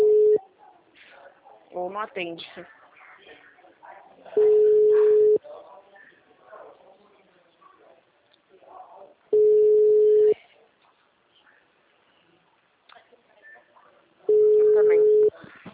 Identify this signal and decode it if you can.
Ringback tone (Europe, 425 Hz), with speech.
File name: out- -2005- - - .48601